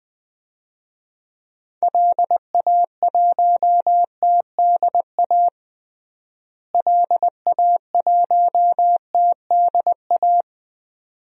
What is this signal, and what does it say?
Morse code, heard as a keyed tone, about 20 wpm, 700 Hz. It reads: LA1TDA LA1TDA
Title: DXCC-100 Call Signs - 6 Characters - Repeated 2x - 20wpm